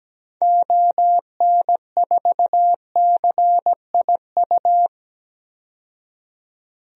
Morse code, heard as a keyed tone, about 17 wpm, 700 Hz. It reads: ON4CIU
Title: DXCC-100 Call Signs - 6 Characters - 17wpm